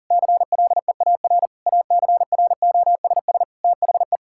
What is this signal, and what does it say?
Morse code, heard as a keyed tone, about 37 wpm, 700 Hz. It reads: CLEAR ACROSS THE
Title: Sentences from Top 1000 Words - Review Part 5 - 40wpm